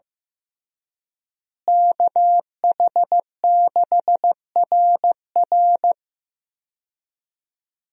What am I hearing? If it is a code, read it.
Morse code, heard as a keyed tone, about 14 wpm, 700 Hz. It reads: KH6RR
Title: USA Call Signs - 5 Characters - Repeated 2x - 15wpm